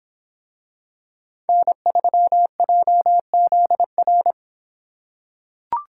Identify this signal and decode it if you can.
Morse code, heard as a keyed tone, about 24 wpm, 700 Hz. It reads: N3JZR